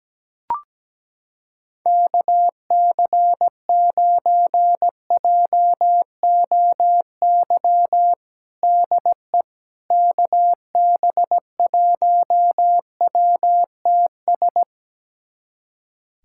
Morse code, heard as a keyed tone, about 17 wpm, 700 Hz. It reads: KC9JOY DE KB1WTS
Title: USA Callsign DE Callsign - 6 Characters - 17wpm